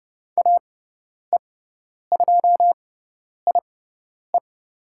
Morse code, heard as a keyed tone, about 30 wpm, spaced as for 10 wpm, 690 Hz. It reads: AE2IE
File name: Lesson 18g - Letter m - Call Signs 30wpm